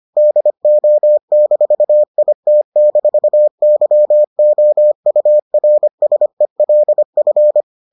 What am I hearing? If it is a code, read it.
Morse code, heard as a keyed tone, about 25 wpm, 600 Hz. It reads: DO-IT-YOURSELF